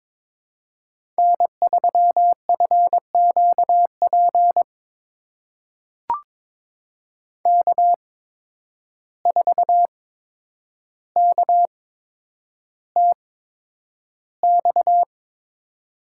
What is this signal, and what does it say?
Morse code, heard as a keyed tone, about 22 wpm, spaced as for 9 wpm, 700 Hz. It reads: N3FQP K4KTX
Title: USA Call Signs - 5 Characters - 8x Character Spacing - 22wpm